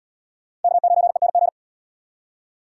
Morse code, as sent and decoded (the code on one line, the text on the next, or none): -.- ----. -. --.
K9ENG